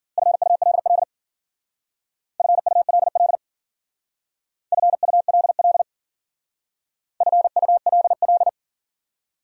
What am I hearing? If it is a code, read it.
Morse code, heard as a keyed tone, about 50 wpm, 700 Hz. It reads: FULL FULL FULL FULL
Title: Speed-Racing Top 500 Words 40wpm